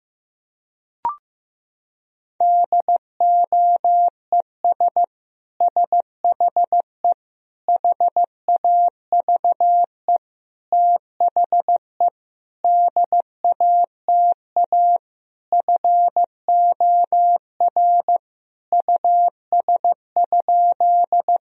Morse code, heard as a keyed tone, about 15 wpm, 700 Hz. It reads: DOES SHE HAVE THE DATA FOR US?
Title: Sentences from Top 1000 Words - Review Part 3 - 15wpm